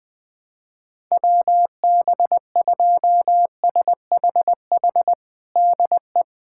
Morse code, heard as a keyed tone, about 20 wpm, 700 Hz. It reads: WB2SHH DE
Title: USA Callsign DE Callsign - 6 Characters - 20wpm